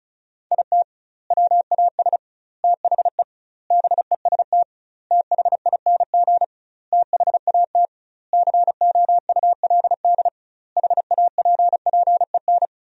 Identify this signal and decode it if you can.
Morse code, heard as a keyed tone, about 36 wpm, 700 Hz. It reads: IT WAS THE BEST THING THAT COULD HAPPEN
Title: 2x Word-Spacing - Sentences from Top 400 Words - 35wpm